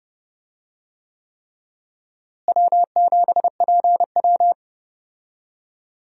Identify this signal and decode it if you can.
Morse code, heard as a keyed tone, about 32 wpm, 700 Hz. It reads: W7PW
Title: USA Call Signs - 4 Characters - Repeated 3x - 30wpm